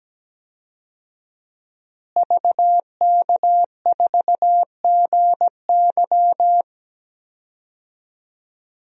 Morse code, heard as a keyed tone, about 17 wpm, 700 Hz. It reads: VK4GY
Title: DXCC-100 Call Signs - 5 Characters - Repeated 2x - 17wpm